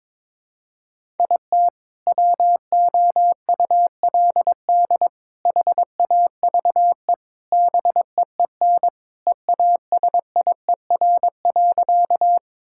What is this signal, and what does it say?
Morse code, heard as a keyed tone, about 22 wpm, 700 Hz. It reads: IT WOULD HAVE BEEN EASIER.
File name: practice / 22wpm / Sentences with 3-Word Phrases - 22wpm